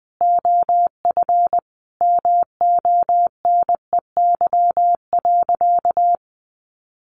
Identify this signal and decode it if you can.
Morse code, heard as a keyed tone, about 20 wpm, 700 Hz. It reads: OF MONEY.
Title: Sentences from Top 1100 Words 20wpm